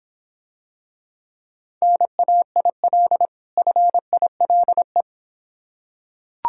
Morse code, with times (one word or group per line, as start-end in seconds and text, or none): 1.8-3.3 NAIL
3.6-5.0 FILE